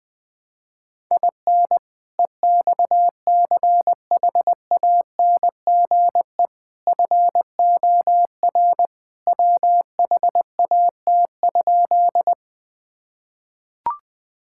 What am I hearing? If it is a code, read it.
Morse code, heard as a keyed tone, about 20 wpm, 700 Hz. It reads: IN EXCHANGE FOR WHAT?
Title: Sentences from Top 1000 Words - Review Part 6 - 20wpm